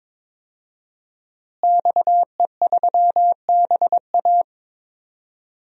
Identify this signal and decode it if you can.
Morse code, heard as a keyed tone, about 22 wpm, 700 Hz. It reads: XE3BA
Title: DXCC-100 Call Signs - 5 Characters - Repeated 3x - 22wpm